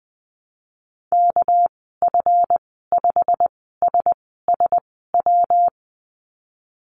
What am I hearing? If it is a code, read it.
Morse code, heard as a keyed tone, about 19 wpm, spaced as for 14 wpm, 700 Hz. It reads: KF5SSW
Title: 207 Lesson 33g - Letter j - Call Signs 20wpm